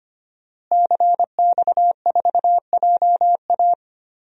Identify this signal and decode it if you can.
Morse code, heard as a keyed tone, about 25 wpm, 700 Hz. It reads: CX4JA
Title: DXCC-100 Call Signs - 5 Characters - Repeated 2x - 25wpm